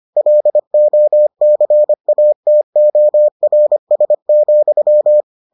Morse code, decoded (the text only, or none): LOCATORS,